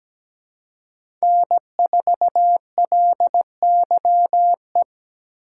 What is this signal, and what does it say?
Morse code, heard as a keyed tone, about 20 wpm, 700 Hz. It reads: N4LYE
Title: Speed Racing - USA 1x3 Call Signs 17wpm